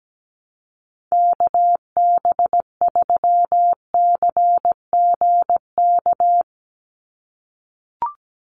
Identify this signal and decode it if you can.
Morse code, heard as a keyed tone, about 17 wpm, 700 Hz. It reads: KB3CGK
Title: USA Call Signs 17wpm